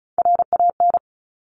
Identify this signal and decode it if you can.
Morse code, heard as a keyed tone, about 36 wpm, 690 Hz. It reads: RAN
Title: Top Three Letter Combinations in Words 35wpm